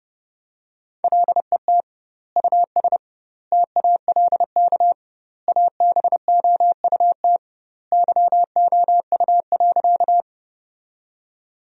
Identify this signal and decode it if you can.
Morse code, heard as a keyed tone, about 30 wpm, 700 Hz. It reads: LET US TALK ABOUT YOU.
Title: 2x Word-Spacing - Sentences from Top 400 Words - 30wpm